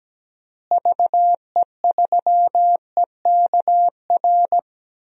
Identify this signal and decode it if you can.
Morse code, heard as a keyed tone, about 17 wpm, 700 Hz. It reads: VE3EKR
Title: DXCC-100 Call Signs - 6 Characters - Repeated 3x - 17wpm